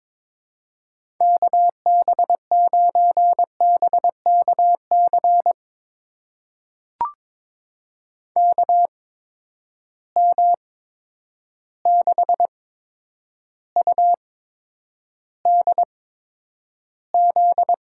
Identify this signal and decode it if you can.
Morse code, heard as a keyed tone, about 22 wpm, spaced as for 9 wpm, 700 Hz. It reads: KB9BKC KM6UDZ
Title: USA Call Signs - 6 Characters - 8x Character Spacing - 22wpm